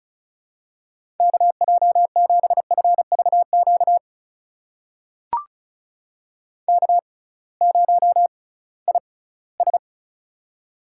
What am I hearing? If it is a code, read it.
Morse code, heard as a keyed tone, about 36 wpm, spaced as for 21 wpm, 700 Hz. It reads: KJ7FVQ K0IS
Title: USA Call Signs - 6x Character Spacing - 35wpm